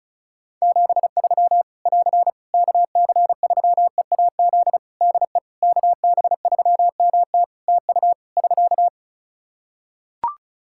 Morse code, heard as a keyed tone, about 34 wpm, 700 Hz. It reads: 73 <AR> KC3EAZ DE KB3MT TU <SK>
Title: Practice QSO - Exchange 4 - Ending - 35wpm